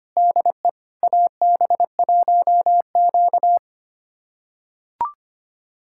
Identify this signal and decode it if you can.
Morse code, heard as a keyed tone, about 25 wpm, 700 Hz. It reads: DE AB1Q
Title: USA Callsign DE Callsign - 4 Characters - 25wpm